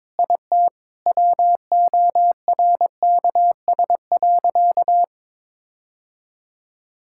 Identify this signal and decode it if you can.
Morse code, heard as a keyed tone, about 22 wpm, 700 Hz. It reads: IT WORKS.
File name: practice / 22wpm / Sentences with 4-Word Phrases - 22wpm